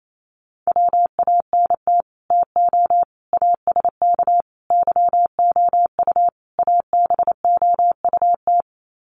Morse code, heard as a keyed tone, about 28 wpm, 700 Hz. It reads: WANT TO ASK YOU ABOUT
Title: Sets of 5 Words Encore2 28wpm